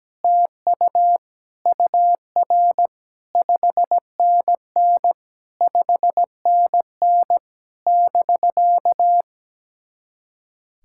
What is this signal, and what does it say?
Morse code, heard as a keyed tone, about 17 wpm, 700 Hz. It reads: TU UR 5NN 5NN <BK>
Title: SOTA QSO Practice 17wpm